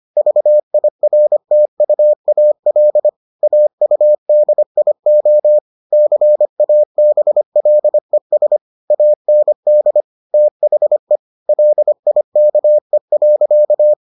Morse code, heard as a keyed tone, about 25 wpm, 600 Hz. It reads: VIRTUAL AUDIO CABLES AND THE LIKE.